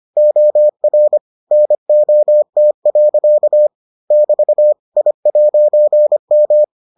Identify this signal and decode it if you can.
Morse code, heard as a keyed tone, about 25 wpm, 600 Hz. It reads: OR NOT. <BT> I'M